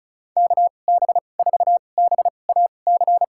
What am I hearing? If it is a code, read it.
Morse code, heard as a keyed tone, about 34 wpm, spaced as for 25 wpm, 700 Hz. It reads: KB4BAC
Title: USA Call Signs - 6 Characters - 2x Character Spacing - 35wpm